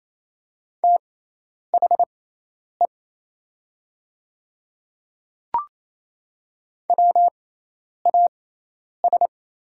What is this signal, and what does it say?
Morse code, heard as a keyed tone, about 28 wpm, spaced as for 10 wpm, 700 Hz. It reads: THE WAS
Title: 6x Character Spacing - Top 100 Words - 28wpm